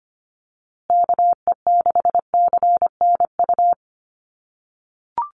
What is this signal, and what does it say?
Morse code, heard as a keyed tone, about 25 wpm, 700 Hz. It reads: KE6CNU
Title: USA Call Signs Repeated 2x 25wpm